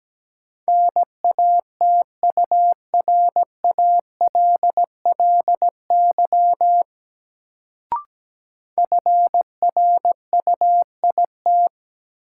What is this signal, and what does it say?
Morse code, heard as a keyed tone, about 17 wpm, 700 Hz. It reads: NATURALLY FRUIT
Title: Sentences from Top 1300 Words - New Word Drill - 17wpm